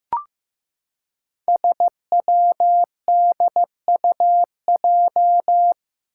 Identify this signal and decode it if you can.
Morse code, heard as a keyed tone, about 16 wpm, 700 Hz. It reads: SWDUJ
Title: Five Challenging Letter-Number Combinations 15wpm